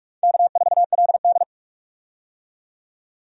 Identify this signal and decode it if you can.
Morse code, heard as a keyed tone, about 43 wpm, 700 Hz. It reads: K4LD
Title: USA Call Signs - 4 Characters - 45wpm